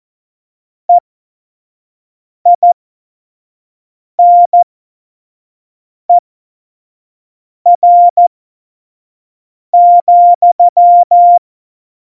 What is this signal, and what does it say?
Morse code, heard as a keyed tone, about 14 wpm, spaced as for 5 wpm, 700 Hz. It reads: EINER,
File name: Morsekurs / Witze / Witz 26. September 17, Tempo 25